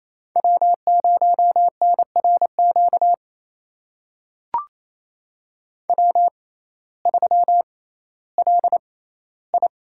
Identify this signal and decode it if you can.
Morse code, heard as a keyed tone, about 27 wpm, spaced as for 15 wpm, 700 Hz. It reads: W0NRQ W3LI
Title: USA Call Signs - 6x Character Spacing - 28wpm